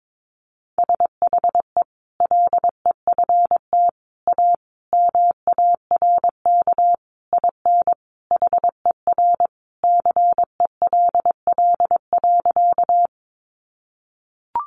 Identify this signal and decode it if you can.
Morse code, heard as a keyed tone, about 22 wpm, 700 Hz. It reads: SHE LEFT A MARK IN HER CELL.
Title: Sentences from Top 1100 Words 22wpm